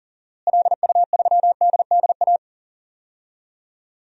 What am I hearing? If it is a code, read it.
Morse code, heard as a keyed tone, about 39 wpm, 700 Hz. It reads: LU3DDA
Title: DXCC-100 Call Signs - 6 Characters - Repeated 2x - 40wpm